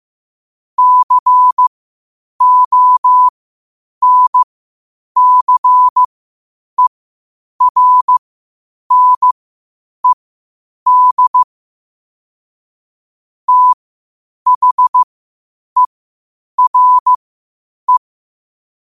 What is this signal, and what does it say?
Morse code, heard as a keyed tone, about 15 wpm, spaced as for 8 wpm, 1000 Hz. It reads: CONCERNED THERE
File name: qotdfiles / 8wpm